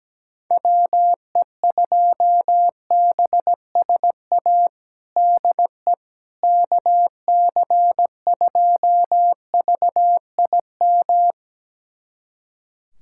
Morse code, heard as a keyed tone, about 17 wpm, 700 Hz. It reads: WE2BSA DE KC2VIM
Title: USA Callsign DE Callsign - 6 Characters - 17wpm